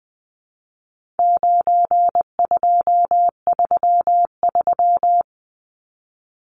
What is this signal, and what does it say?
Morse code, heard as a keyed tone, about 20 wpm, 700 Hz. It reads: 9233
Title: Numbers - 4-Digit 20wpm